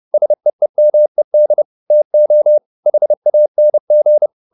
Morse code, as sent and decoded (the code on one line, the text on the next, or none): ... . -- . -.. / - --- / .... .- -. --.
SEEMED TO HANG